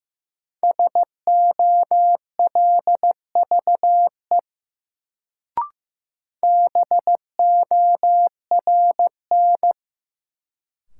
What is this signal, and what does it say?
Morse code, heard as a keyed tone, about 15 wpm, 700 Hz. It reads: SOLVE BORN
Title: Sentences from Top 1000 Words - New Word Drill - 15wpm